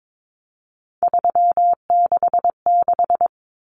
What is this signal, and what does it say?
Morse code, heard as a keyed tone, about 22 wpm, 700 Hz. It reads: 366
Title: Common Cut Numbers - 3-Digit 22wpm